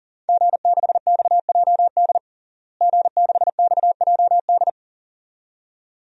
Morse code, heard as a keyed tone, about 40 wpm, 700 Hz. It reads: G6XJD G6XJD
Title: DXCC-100 Call Signs - 5 Characters - Repeated 2x - 40wpm